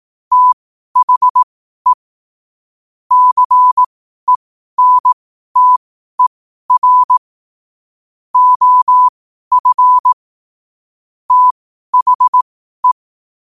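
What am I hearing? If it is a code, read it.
Morse code, heard as a keyed tone, about 18 wpm, spaced as for 12 wpm, 1000 Hz. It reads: THE CENTER OF THE
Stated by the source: Morse code at 12 WPM.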